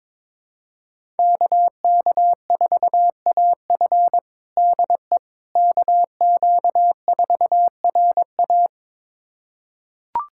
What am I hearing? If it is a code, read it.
Morse code, heard as a keyed tone, about 22 wpm, 700 Hz. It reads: KK4AF DE KQ4RA
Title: USA Callsign DE Callsign - 5 Characters - 22wpm